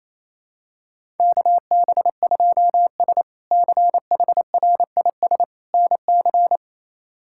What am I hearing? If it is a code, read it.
Morse code, heard as a keyed tone, about 28 wpm, 700 Hz. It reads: KB2S CHRIS NC
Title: K1USN SST QSO Practice - 28wpm